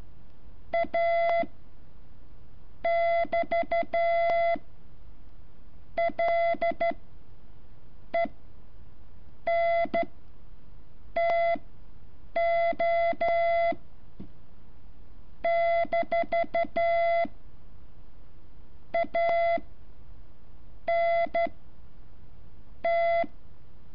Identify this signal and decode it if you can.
Morse code, heard as a keyed tone, about 13 wpm, spaced as for 6 wpm, 680 Hz. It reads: A<BT>LENTO - ANT
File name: curso / ejercicio 14